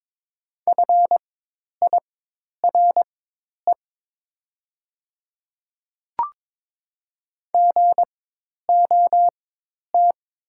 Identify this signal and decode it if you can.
Morse code, heard as a keyed tone, about 22 wpm, spaced as for 10 wpm, 700 Hz. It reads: FIRE GOT
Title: 4x Character Spacing - Top 500 Words - 22wpm